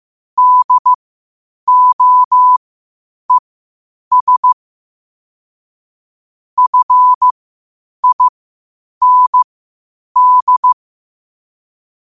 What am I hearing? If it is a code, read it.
Morse code, heard as a keyed tone, about 15 wpm, spaced as for 8 wpm, 1000 Hz. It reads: DOES FIND
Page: 8 wpm morse code quotes